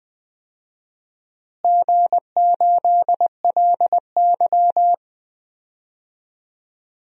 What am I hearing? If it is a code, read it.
Morse code, heard as a keyed tone, about 20 wpm, 700 Hz. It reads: G8LY
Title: DXCC-100 Call Signs - 4 Characters - Repeated 2x - 20wpm